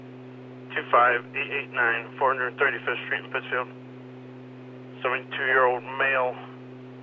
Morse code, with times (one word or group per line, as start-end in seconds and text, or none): none